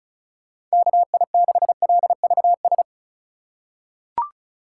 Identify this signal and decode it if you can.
Morse code, heard as a keyed tone, about 34 wpm, 690 Hz. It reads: KI6LVS
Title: USA Call Signs - 3x Character Spacing - 35wpm